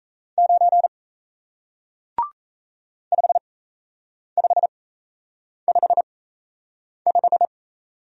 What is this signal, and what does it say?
Morse code, heard as a keyed tone, about 37 wpm, spaced as for 8 wpm, 700 Hz. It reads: 9 5555